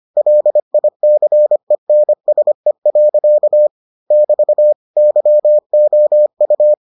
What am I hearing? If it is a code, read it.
Morse code, heard as a keyed tone, about 25 wpm, 600 Hz. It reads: LICENSE. <BT> YOU